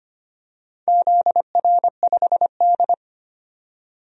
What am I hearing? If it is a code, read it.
Morse code, heard as a keyed tone, about 25 wpm, 700 Hz. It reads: ZR5D
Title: DXCC-100 Call Signs - 4 Characters - 25wpm